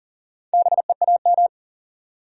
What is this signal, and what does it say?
Morse code, heard as a keyed tone, about 40 wpm, 700 Hz. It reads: BEAM
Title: Top 90 QSO Elements 40wpm